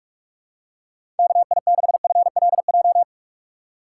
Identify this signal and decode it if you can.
Morse code, heard as a keyed tone, about 46 wpm, 700 Hz. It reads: KI6FLJ
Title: USA Call Signs - 6 Characters - 6x Character Spacing - 45wpm